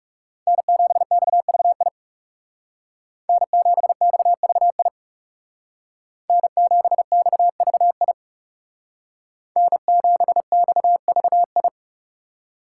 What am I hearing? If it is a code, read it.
Morse code, heard as a keyed tone, about 38 wpm, 700 Hz. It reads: N7XVI N7XVI N7XVI N7XVI